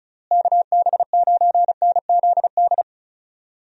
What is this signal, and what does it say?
Morse code, heard as a keyed tone, about 36 wpm, 700 Hz. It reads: KB9NZD
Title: USA Call Signs - 6 Characters - Repeated 2x - 35wpm